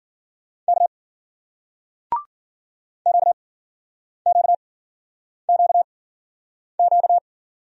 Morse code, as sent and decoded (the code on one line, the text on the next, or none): -.- / --.- --.- --.- --.-
K QQQQ